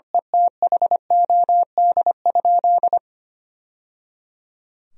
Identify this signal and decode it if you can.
Morse code, heard as a keyed tone, about 25 wpm, 700 Hz. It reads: ETHOD?
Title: Sentences from Top 1000 Words - Review Part 3 - 25wpm